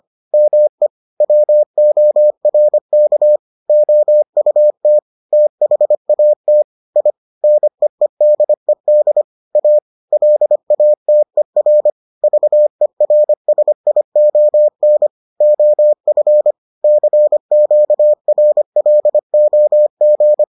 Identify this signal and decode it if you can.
Morse code, heard as a keyed tone, about 25 wpm, 600 Hz. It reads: ME WORK OUT THAT I NEEDED A LATER VERSION OF CQRLOG